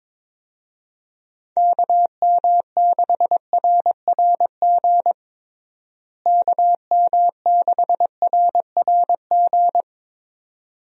Morse code, heard as a keyed tone, about 21 wpm, 700 Hz. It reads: KM6RRG KM6RRG
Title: USA Call Signs - 6 Characters - Repeated 2x - 22wpm